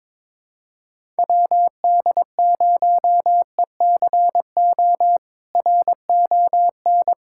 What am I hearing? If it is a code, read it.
Morse code, heard as a keyed tone, about 22 wpm, 700 Hz. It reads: WD0ECO RON
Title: K1USN SST QSO Practice - 22wpm